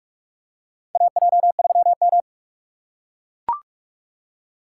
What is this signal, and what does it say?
Morse code, heard as a keyed tone, about 43 wpm, 700 Hz. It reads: AJ3M
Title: USA Call Signs - 3x Character Spacing - 45wpm